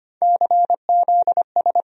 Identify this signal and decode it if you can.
Morse code, heard as a keyed tone, about 25 wpm, 700 Hz. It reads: CZS
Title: USA Call Sign Postfixes - 3 Letters 25wpm